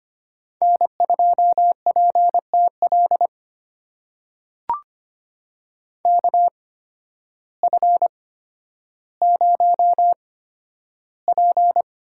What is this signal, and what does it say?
Morse code, heard as a keyed tone, about 25 wpm, spaced as for 12 wpm, 700 Hz. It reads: N2PTL KF0P
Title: USA Call Signs - 5 Characters - 8x Character Spacing - 25wpm